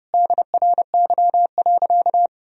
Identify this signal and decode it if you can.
Morse code, heard as a keyed tone, about 30 wpm, 700 Hz. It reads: DRY.
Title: Sentences from Top 1000 Words - Review Part 3 - 30wpm